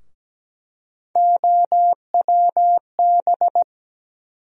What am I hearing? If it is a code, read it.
Morse code, heard as a keyed tone, about 17 wpm, 700 Hz. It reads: OWB
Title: USA Call Sign Postfixes - 3 Letters 17wpm